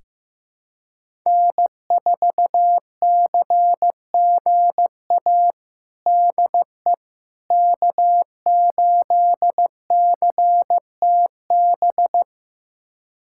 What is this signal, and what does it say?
Morse code, heard as a keyed tone, about 15 wpm, 700 Hz. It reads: N4CGA DE K8CTB
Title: USA Callsign DE Callsign - 5 Characters - 15wpm